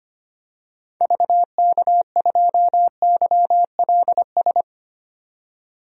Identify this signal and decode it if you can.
Morse code, heard as a keyed tone, about 25 wpm, 700 Hz. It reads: VK2YLS
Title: DXCC-100 Call Signs - 6 Characters - 25wpm